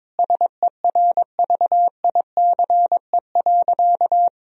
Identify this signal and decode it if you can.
Morse code, heard as a keyed tone, about 22 wpm, 700 Hz. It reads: SERVICE.
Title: Sentences from Top 1000 Words - Review Part 4 - 22wpm